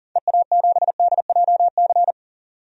Morse code, heard as a keyed tone, about 39 wpm, 700 Hz. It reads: EA7DJC
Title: DXCC-100 Call Signs - 6 Characters - 40wpm